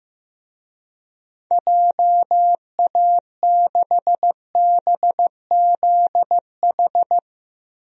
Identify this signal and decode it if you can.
Morse code, heard as a keyed tone, about 15 wpm, 700 Hz. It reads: JA6BZH
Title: DXCC-100 Call Signs - 6 Characters - Repeated 2x - 15wpm